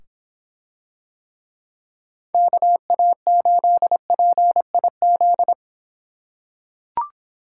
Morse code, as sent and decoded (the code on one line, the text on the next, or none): -.- .- ---.. .--. .. --..
KA8PIZ